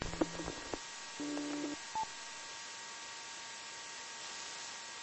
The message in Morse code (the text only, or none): none